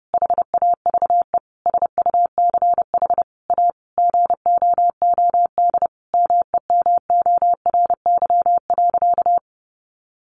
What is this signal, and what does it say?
Morse code, heard as a keyed tone, about 28 wpm, 700 Hz. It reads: HAVE SUCH A GOOD MEMORY.
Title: Sentences from Top 1300 Words 30wpm